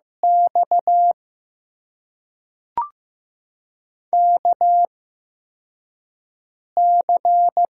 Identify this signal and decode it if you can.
Morse code, heard as a keyed tone, about 15 wpm, spaced as for 4 wpm, 700 Hz. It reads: X KC
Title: USA Call Signs - 8x Character Spacing - 15wpm